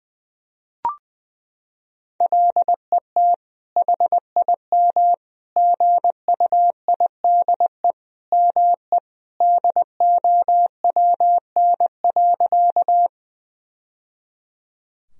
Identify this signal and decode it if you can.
Morse code, heard as a keyed tone, about 20 wpm, 700 Hz. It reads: LET HIM GUIDE ME DOWN.
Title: Sentences from Top 1000 Words - Review Part 3 - 20wpm